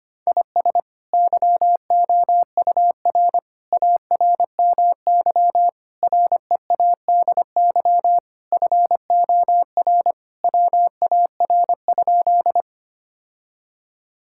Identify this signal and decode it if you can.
Morse code, heard as a keyed tone, about 25 wpm, 700 Hz. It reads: IS YOUR ARMY READY FOR WAR?
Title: Sentences from Top 1000 Words - Review Part 5 - 25wpm